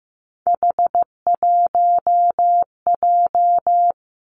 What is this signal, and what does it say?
Morse code, heard as a keyed tone, about 15 wpm, 700 Hz. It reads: H1J
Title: Three Challenging Letter-Number Combinations 15wpm